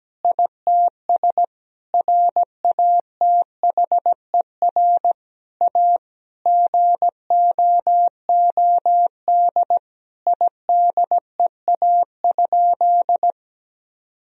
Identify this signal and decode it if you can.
Morse code, heard as a keyed tone, about 17 wpm, 700 Hz. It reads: ITS RATHER A GOOD IDEA?
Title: Sentences from Top 1000 Words - Review Part 4 - 17wpm